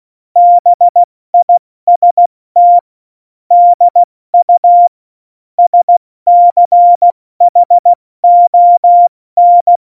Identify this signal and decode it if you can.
Morse code, heard as a keyed tone, about 16 wpm, spaced as for 14 wpm, 700 Hz. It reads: BIST DU SCHON
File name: Morsekurs / Witze / Witz 10. Oktober 17, Tempo 70